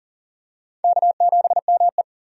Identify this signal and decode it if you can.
Morse code, heard as a keyed tone, about 40 wpm, 700 Hz. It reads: K7ME
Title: USA Call Signs - 4 Characters - 40wpm